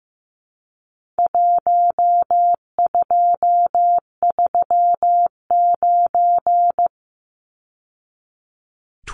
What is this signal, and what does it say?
Morse code, heard as a keyed tone, about 15 wpm, 700 Hz. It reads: 1239
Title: Numbers - 4-Digit 15wpm